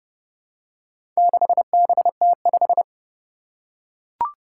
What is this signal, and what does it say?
Morse code, heard as a keyed tone, about 29 wpm, 700 Hz. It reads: 6BT5
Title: Four Challenging Letter-Number Combinations 30wpm